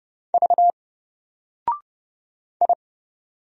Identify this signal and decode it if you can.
Morse code, heard as a keyed tone, about 29 wpm, 700 Hz. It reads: V I
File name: Single-Letter-Number-30wpm